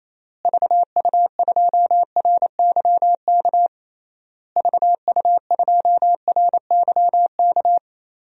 Morse code, heard as a keyed tone, about 28 wpm, 700 Hz. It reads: VU2RYK VU2RYK
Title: DXCC-100 Call Signs - 6 Characters - Repeated 3x - 28wpm